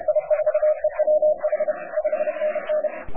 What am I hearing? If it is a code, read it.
Morse code, heard as a keyed tone, about 31 wpm, 610 Hz. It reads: I4EWH 5NN
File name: K4M cw 80m